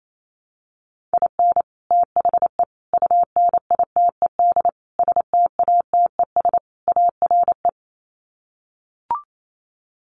Morse code, heard as a keyed tone, about 28 wpm, 700 Hz. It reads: IN THE UNITED STATES ARE
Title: Sets of 5 Words Encore 28wpm